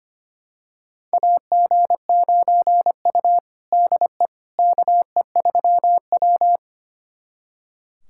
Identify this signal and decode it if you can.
Morse code, heard as a keyed tone, about 25 wpm, 700 Hz. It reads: AG9U DE KE3W
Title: USA Callsign DE Callsign - 4 Characters - 25wpm